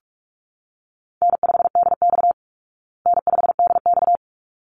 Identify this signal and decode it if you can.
Morse code, heard as a keyed tone, about 44 wpm, 700 Hz. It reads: N5DX N5DX
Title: Call Signs Repeated 3x - 45wpm